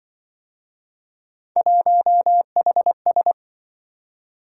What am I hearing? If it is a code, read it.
Morse code, heard as a keyed tone, about 25 wpm, 700 Hz. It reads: 1HS